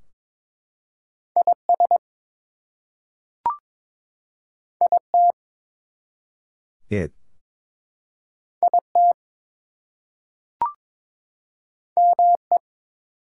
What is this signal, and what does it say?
Morse code, heard as a keyed tone, about 22 wpm, 700 Hz. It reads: IS IT IT ME
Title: Two Letter Words - 22wpm